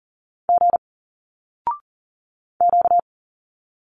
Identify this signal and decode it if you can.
Morse code, heard as a keyed tone, about 40 wpm, 700 Hz. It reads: G Q
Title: Single Letters 40wpm